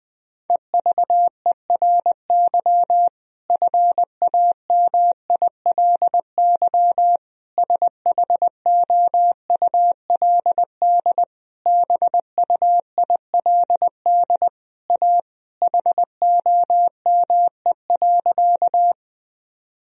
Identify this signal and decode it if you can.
Morse code, heard as a keyed tone, about 20 wpm, 700 Hz. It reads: EVERY FAMILY SHOULD BUILD A HOME.
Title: Sentences from Top 1000 Words - Review Part 1 - 20wpm